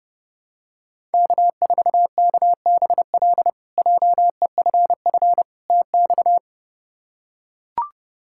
Morse code, heard as a keyed tone, about 30 wpm, 700 Hz. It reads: K4KBL JEFF TX
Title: K1USN SST QSO Practice - 30wpm